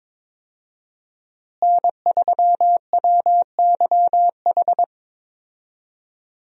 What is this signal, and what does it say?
Morse code, heard as a keyed tone, about 22 wpm, 700 Hz. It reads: N3WYH
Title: USA Call Signs - 5 Characters - Repeated 3x - 22wpm